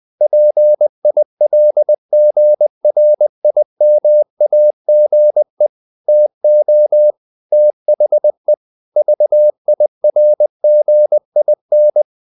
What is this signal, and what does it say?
Morse code, heard as a keyed tone, about 20 wpm, 600 Hz. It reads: PILGRIMAGE TO THE VIRGIN
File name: the old man and the sea / the old man and the sea part 1 20wpm